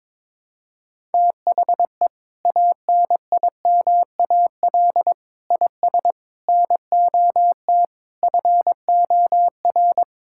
Morse code, heard as a keyed tone, about 22 wpm, 700 Hz. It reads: THE ANIMAL IS NOT FOR
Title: Sentences from Top 1000 Words - Review Part 5 - 22wpm